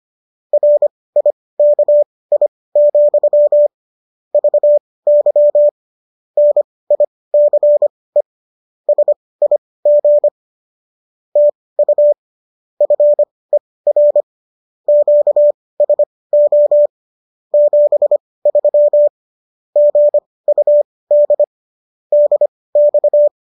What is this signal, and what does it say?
Morse code, heard as a keyed tone, about 25 wpm, spaced as for 18 wpm, 590 Hz. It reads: RIKI, VY NICE SIG TU FER QSO 73 GUD DX